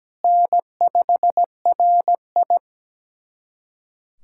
Morse code, heard as a keyed tone, about 17 wpm, 700 Hz. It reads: N5RI
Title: USA Callsign DE Callsign - 4 Characters - 17wpm